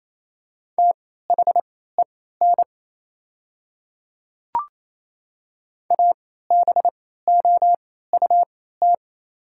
Morse code, heard as a keyed tone, about 28 wpm, spaced as for 16 wpm, 700 Hz. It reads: THEN ABOUT
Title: 3x Character Spacing - Top 200 Words - 28wpm